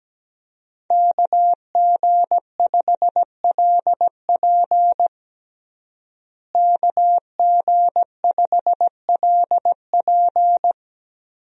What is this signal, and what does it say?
Morse code, heard as a keyed tone, about 17 wpm, 700 Hz. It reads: KG5LP KG5LP
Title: USA Call Signs - 5 Characters - Repeated 3x - 17wpm